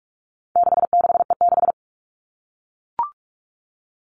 Morse code, heard as a keyed tone, about 44 wpm, 700 Hz. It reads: 66E6
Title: All Cut Numbers - 4-Digit 45wpm